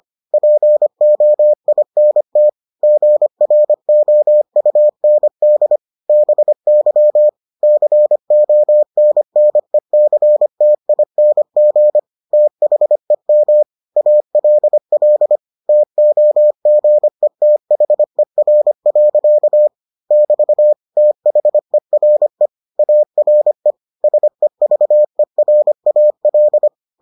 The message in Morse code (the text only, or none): POINT GROUND BY CONNECTING THEM ALL TOGETHER. <BT> THERE ARE SEVERAL